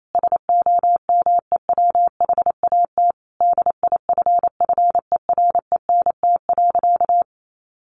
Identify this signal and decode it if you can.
Morse code, heard as a keyed tone, about 28 wpm, 700 Hz. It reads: SOMEWHAT DIFFERENT.
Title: Sentences from Top 1300 Words 28wpm